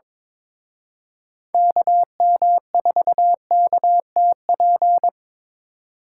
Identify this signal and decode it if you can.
Morse code, heard as a keyed tone, about 25 wpm, 700 Hz. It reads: KM4KTP
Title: Speed Racing - USA 2x3 Call Signs 22wpm